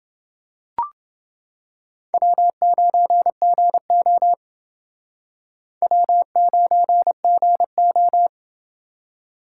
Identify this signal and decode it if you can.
Morse code, heard as a keyed tone, about 31 wpm, 700 Hz. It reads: W9GO W9GO